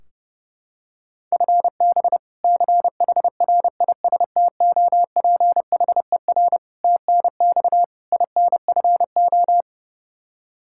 Morse code, heard as a keyed tone, about 30 wpm, 700 Hz. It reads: FB CHRISTOPHER TNX INFO
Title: K1USN SST QSO Practice 30wpm